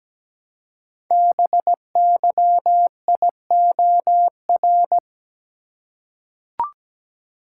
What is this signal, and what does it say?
Morse code, heard as a keyed tone, about 17 wpm, 700 Hz. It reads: BYIOR
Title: Code Groups - 5 Letters - 17wpm